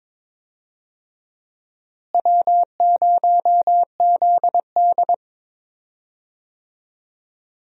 Morse code, heard as a keyed tone, about 22 wpm, 700 Hz. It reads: W0ZD
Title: USA Call Signs - 4 Characters - Repeated 2x - 22wpm